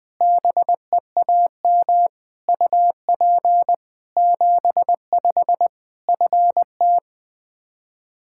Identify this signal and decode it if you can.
Morse code, heard as a keyed tone, about 21 wpm, 700 Hz. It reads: BEAM UP 75 FT
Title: Practice QSO - Exchange 2 - Info - 20wpm